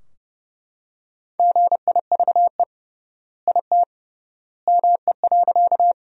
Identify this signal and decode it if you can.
Morse code, heard as a keyed tone, about 30 wpm, 700 Hz. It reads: GIVE IT ME.
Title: 3x Word-Spacing - Sentences from Top 200 Words - 30wpm